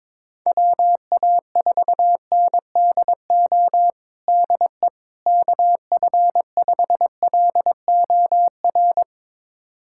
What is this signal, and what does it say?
Morse code, heard as a keyed tone, about 22 wpm, 700 Hz. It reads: WA4NDO DE KF5LOR
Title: USA Callsign DE Callsign 22wpm